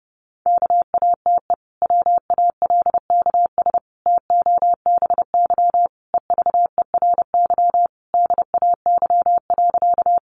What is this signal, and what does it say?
Morse code, heard as a keyed tone, about 30 wpm, 700 Hz. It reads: KATE WALKS TOBY EVERY DAY.
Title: Easy Phrases 30wpm